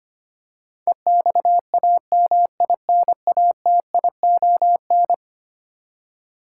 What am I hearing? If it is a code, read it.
Morse code, heard as a keyed tone, about 25 wpm, 700 Hz. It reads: EXAMINATION
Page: morse-code-ninja-practice-25wpm